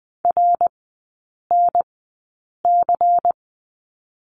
Morse code, heard as a keyed tone, about 20 wpm, spaced as for 8 wpm, 700 Hz. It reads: RNC
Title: 133 Lesson 22e - Question Mark - Three Characters 20wpm